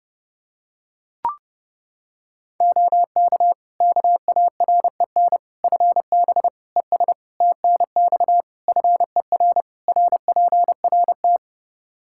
Morse code, heard as a keyed tone, about 30 wpm, 700 Hz. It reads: OK KAREN FB ES TNX FER RPRT
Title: Practice QSO - Exchange 2 - Info - 30wpm